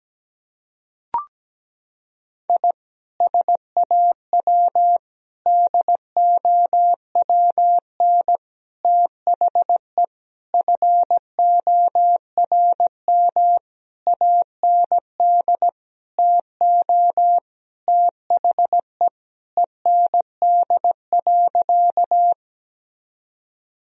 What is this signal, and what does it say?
Morse code, heard as a keyed tone, about 17 wpm, 700 Hz. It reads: I SAW DOWN THE FORM AND TO THE END.
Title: Sentences from Top 1000 Words - Review Part 1 - 17wpm